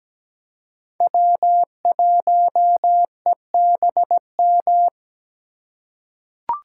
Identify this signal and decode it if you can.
Morse code, heard as a keyed tone, about 17 wpm, 700 Hz. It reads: W1EBM
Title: K1USN SST QSO Practice - 17wpm